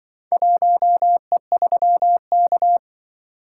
Morse code, heard as a keyed tone, about 24 wpm, 700 Hz. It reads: 1E3K
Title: Speed-Racing - Four Letter-Number Combinations - 20wpm